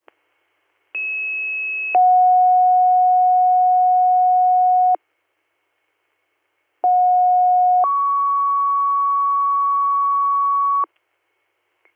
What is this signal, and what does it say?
Two-tone page, heard as a one-second tone, then a three-second tone, about 2600 then 730 Hz, about 730 then 1120 Hz. It is heard from 0.9 to 10.9 s.